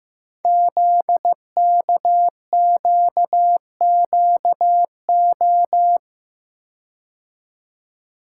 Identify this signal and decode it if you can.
Morse code, heard as a keyed tone, about 15 wpm, 700 Hz. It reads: ZKQQO
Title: Code Groups - 5 Letters - 15wpm